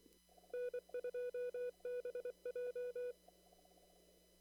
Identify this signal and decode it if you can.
Morse code, heard as a keyed tone, about 24 wpm, 490 Hz. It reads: N2BJ
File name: CW / kt5j wpx cw-op-k5pi